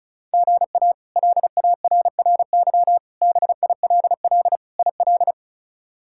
Morse code, heard as a keyed tone, about 36 wpm, 700 Hz. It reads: GA LARRY BILL IL
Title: K1USN SST QSO Practice - 35wpm